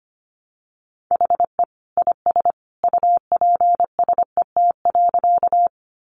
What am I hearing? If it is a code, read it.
Morse code, heard as a keyed tone, about 25 wpm, 700 Hz. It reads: HE IS UPSET.
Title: Additional Easy Phrases 25wpm